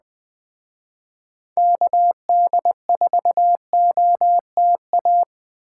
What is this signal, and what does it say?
Morse code, heard as a keyed tone, about 21 wpm, 700 Hz. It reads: KD4OTA